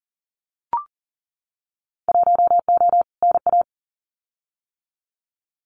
Morse code, heard as a keyed tone, about 38 wpm, 700 Hz. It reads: 1O NA
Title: Winter Field Day Exchanges 40wpm